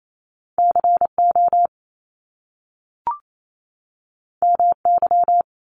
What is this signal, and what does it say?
Morse code, heard as a keyed tone, about 28 wpm, 700 Hz. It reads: CO MY
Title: Top Two Letter Combinations in Words 28wpm